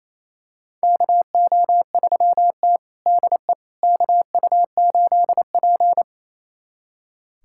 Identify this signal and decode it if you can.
Morse code, heard as a keyed tone, about 28 wpm, 700 Hz. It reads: KO3T DE KU8P
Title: USA Callsign DE Callsign - 4 Characters - 28wpm